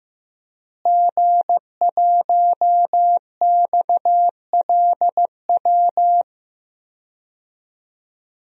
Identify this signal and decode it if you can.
Morse code, heard as a keyed tone, about 15 wpm, 700 Hz. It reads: G1XLW
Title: DXCC-100 Call Signs - 5 Characters - Repeated 2x - 15wpm